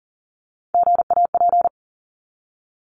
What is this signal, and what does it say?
Morse code, heard as a keyed tone, about 40 wpm, 700 Hz. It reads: GAP
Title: Top 1501-1750 Words 40wpm